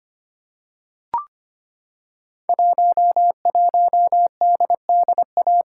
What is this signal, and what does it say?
Morse code, heard as a keyed tone, about 25 wpm, 700 Hz. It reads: 11DDA
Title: Code Groups - 5 Letter:Numbers - 25wpm